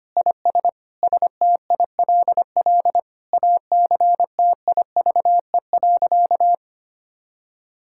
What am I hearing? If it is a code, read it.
Morse code, heard as a keyed tone, about 25 wpm, 700 Hz. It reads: IS STILL ACTIVE.
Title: Sentences from Top 1000 Words - Review Part 6 - 25wpm